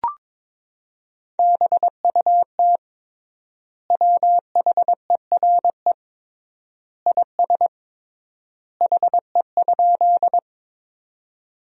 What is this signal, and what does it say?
Morse code, heard as a keyed tone, about 22 wpm, 700 Hz. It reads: BUT WHERE IS HE?
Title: 3x Word-Spacing - Sentences from Top 200 Words - 22wpm